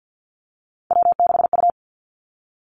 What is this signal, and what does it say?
Morse code, heard as a keyed tone, about 50 wpm, 700 Hz. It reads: W6U
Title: Three Challenging Letter-Number Combinations 50wpm